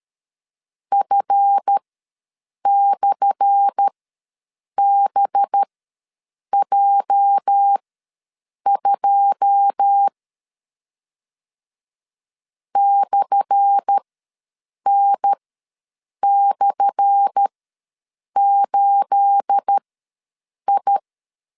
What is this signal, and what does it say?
Morse code, heard as a keyed tone, about 13 wpm, spaced as for 7 wpm, 800 Hz. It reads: F/BJ2 /N/8I
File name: LEZIONI / 10 SIGNS / 10A SLASH ws 12wpm 3443